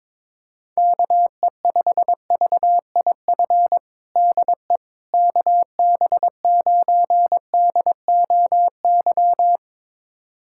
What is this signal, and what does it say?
Morse code, heard as a keyed tone, about 22 wpm, 700 Hz. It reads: KE5VIF DE KB9DOY
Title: USA Callsign DE Callsign - 6 Characters - 22wpm